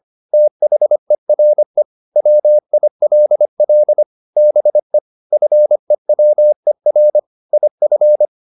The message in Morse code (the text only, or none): THERE WILL BE FEWER IF